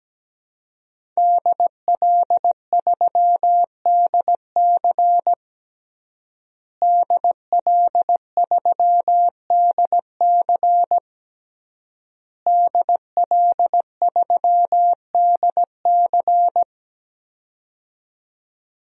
Morse code, heard as a keyed tone, about 17 wpm, 700 Hz. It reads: DL3DC DL3DC DL3DC
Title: DXCC-100 Call Signs - 5 Characters - Repeated 3x - 17wpm